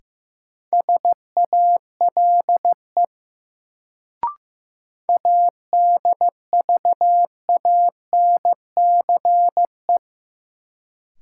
Morse code, heard as a keyed tone, about 16 wpm, 700 Hz. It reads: SALE ADVANCE
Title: Sentences from Top 1400 Words - New Word Drill Alt - 15wpm